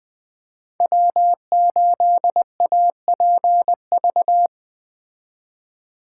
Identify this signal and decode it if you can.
Morse code, heard as a keyed tone, about 21 wpm, 700 Hz. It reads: W8APV